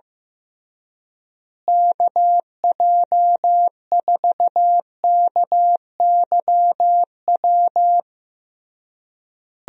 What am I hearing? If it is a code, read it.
Morse code, heard as a keyed tone, about 15 wpm, 700 Hz. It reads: KJ4KYW
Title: USA Call Signs - 6 Characters - Repeated 3x - 15wpm